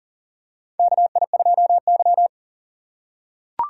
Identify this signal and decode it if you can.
Morse code, heard as a keyed tone, about 40 wpm, 700 Hz. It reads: KI2Y
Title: USA Call Signs - 4 Characters - 2x Character Spacing - 40wpm